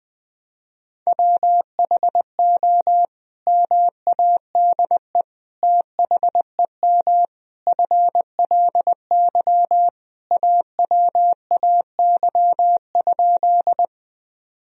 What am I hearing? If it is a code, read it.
Morse code, heard as a keyed tone, about 20 wpm, 700 Hz. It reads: WHO MADE THEM FLY AWAY?
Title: Sentences from Top 1000 Words - Review Part 2 - 20wpm